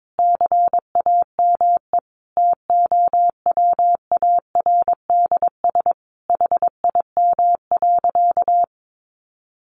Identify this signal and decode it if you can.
Morse code, heard as a keyed tone, about 22 wpm, 700 Hz. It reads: CAME TOWARDS HIM.
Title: Sentences from Top 1100 Words 22wpm